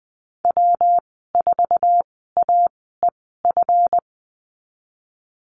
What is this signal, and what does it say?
Morse code, heard as a keyed tone, about 20 wpm, spaced as for 15 wpm, 700 Hz. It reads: W4AEF
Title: 161 Lesson 26g - Letter g - Call Signs 20wpm